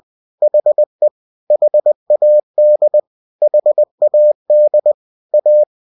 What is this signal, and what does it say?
Morse code, heard as a keyed tone, about 20 wpm, 600 Hz. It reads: HE HAD HAD A
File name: cw-books / winnie-the-pooh / winnie-20wpm-page-0093